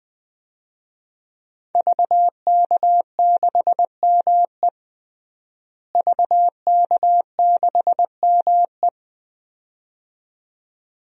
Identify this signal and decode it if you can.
Morse code, heard as a keyed tone, about 20 wpm, 700 Hz. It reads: VK6ME VK6ME
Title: DXCC-100 Call Signs - 5 Characters - Repeated 2x - 20wpm